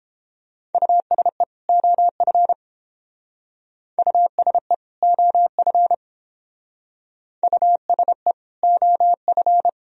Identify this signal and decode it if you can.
Morse code, heard as a keyed tone, about 29 wpm, 700 Hz. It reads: USE OF USE OF USE OF